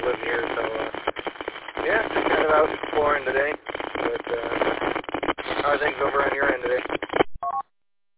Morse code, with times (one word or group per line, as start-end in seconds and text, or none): none